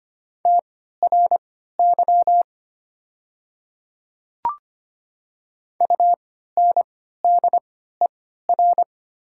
Morse code, heard as a keyed tone, about 25 wpm, spaced as for 14 wpm, 700 Hz. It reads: TRY UNDER
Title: 3x Character Spacing - Top 200 Words - 25wpm